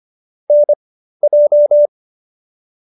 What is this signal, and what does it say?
Morse code, heard as a keyed tone, about 25 wpm, spaced as for 13 wpm, 590 Hz. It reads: NJ